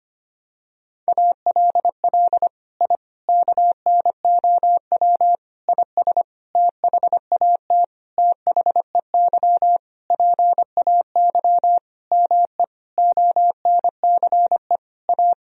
Morse code, heard as a keyed tone, about 25 wpm, 700 Hz. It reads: ALL I KNOW IS THAT THEY PAY ME ONCE A
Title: Sentences from Top 1000 Words - Review Part 1 - 25wpm